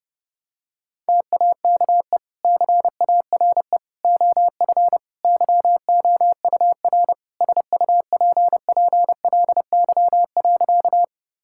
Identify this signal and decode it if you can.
Morse code, heard as a keyed tone, about 30 wpm, 700 Hz. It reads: TAKE CARE OF YOUR SUPPLY.
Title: Sentences from Top 1000 Words - Review Part 5 - 30wpm